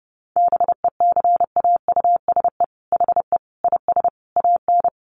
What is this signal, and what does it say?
Morse code, heard as a keyed tone, about 30 wpm, 700 Hz. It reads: BECAUSE HE IS AN
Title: Sentences from Top 1500 Words 30wpm